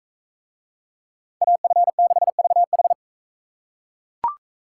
Morse code, heard as a keyed tone, about 44 wpm, 700 Hz. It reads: AF6VH